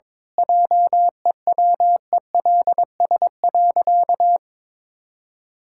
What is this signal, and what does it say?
Morse code, heard as a keyed tone, about 22 wpm, 700 Hz. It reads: JEWELS.
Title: Pangram Challenge v2 - 22wpm